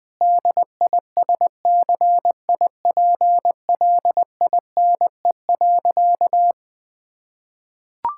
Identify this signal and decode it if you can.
Morse code, heard as a keyed tone, about 20 wpm, 700 Hz. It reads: DISCIPLINE.
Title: Sentences from Top 1000 Words - Review Part 6 - 20wpm